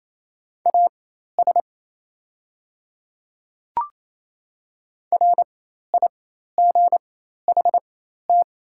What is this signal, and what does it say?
Morse code, heard as a keyed tone, about 28 wpm, spaced as for 13 wpm, 700 Hz. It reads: AS RIGHT
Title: 4x Character Spacing - Top 200 Words - 28wpm